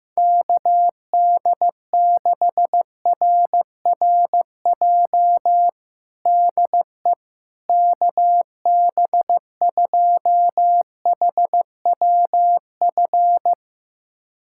Morse code, heard as a keyed tone, about 15 wpm, 700 Hz. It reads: KD6RRJ DE KB2HWF
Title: USA Callsign DE Callsign - 6 Characters - 15wpm